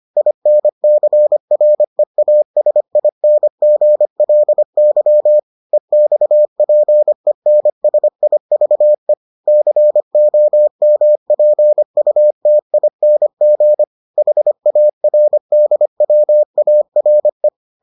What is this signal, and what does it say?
Morse code, heard as a keyed tone, about 25 wpm, 600 Hz. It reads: INCREASINGLY EXPENSIVE COMPUTING HARDWARE